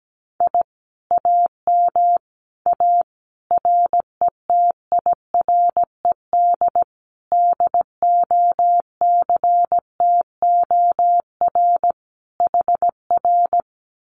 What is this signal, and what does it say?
Morse code, heard as a keyed tone, about 17 wpm, 700 Hz. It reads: I AM A RETIRED DOCTOR HR
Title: Common QSO Phrases Encore 17wpm